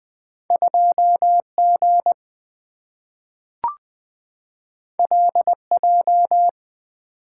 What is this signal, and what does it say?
Morse code, heard as a keyed tone, about 20 wpm, 700 Hz. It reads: 2G LJ
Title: Two Letter-Number Combinations - 20wpm